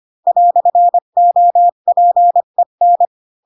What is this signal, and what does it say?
Morse code, heard as a keyed tone, about 25 wpm, 700 Hz. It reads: "OPEN
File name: Pinocchio / Pinocchio ch7 20 25 ik1hge